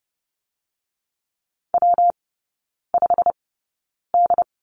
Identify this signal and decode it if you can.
Morse code, heard as a keyed tone, about 30 wpm, 700 Hz. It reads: W 5 D
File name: Lesson 19e - Letter w - Three Characters 30wpm